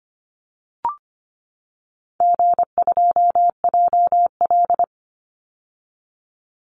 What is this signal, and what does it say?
Morse code, heard as a keyed tone, about 25 wpm, 700 Hz. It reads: G2JL
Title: Call Signs - 25wpm